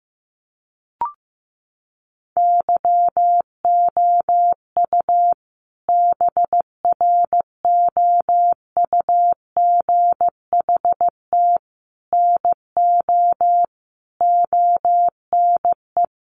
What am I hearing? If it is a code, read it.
Morse code, heard as a keyed tone, about 15 wpm, 700 Hz. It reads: YOU BROUGHT NO ONE
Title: Sentences from Top 500 Words 15wpm